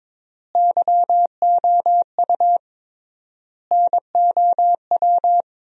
Morse code, heard as a keyed tone, about 22 wpm, 700 Hz. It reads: YOU NOW
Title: 3x Word-Spacing - Sentences from Top 200 Words - 22wpm